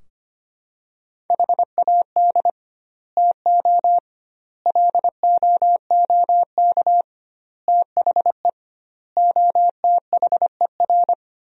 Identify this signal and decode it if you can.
Morse code, heard as a keyed tone, about 25 wpm, 700 Hz. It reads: HAD TO LOOK THE OTHER
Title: 2x Word-Spacing - Sentences from Top 100 Words - 25wpm